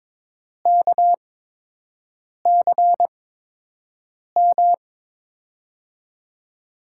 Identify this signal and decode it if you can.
Morse code, heard as a keyed tone, about 22 wpm, spaced as for 6 wpm, 700 Hz. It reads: KCM
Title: USA Call Signs - 6 Characters - 8x Character Spacing - 22wpm